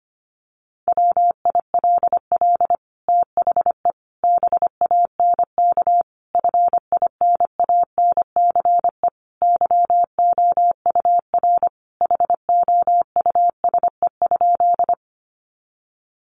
Morse code, heard as a keyed tone, about 25 wpm, 700 Hz. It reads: WILL THE BANK FINANCE YOUR HOUSE?
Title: Sentences from Top 1500 Words 25wpm